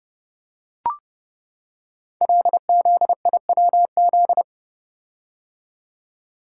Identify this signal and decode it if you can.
Morse code, heard as a keyed tone, about 30 wpm, 700 Hz. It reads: LZIWZ
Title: Code Groups - 5 Letters - 30wpm